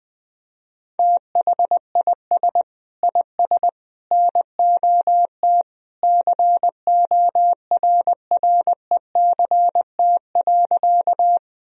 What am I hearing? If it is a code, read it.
Morse code, heard as a keyed tone, about 21 wpm, 700 Hz. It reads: THIS IS NOT CORRECT.
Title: Sentences from Top 1000 Words - Review Part 2 - 20wpm